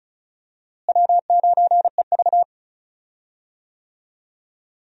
Morse code, heard as a keyed tone, about 36 wpm, 700 Hz. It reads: W9EV
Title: USA Call Signs - 4 Characters - Repeated 2x - 35wpm